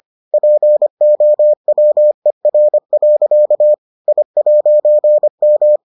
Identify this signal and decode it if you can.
Morse code, heard as a keyed tone, about 25 wpm, 600 Hz. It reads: POWER. I'M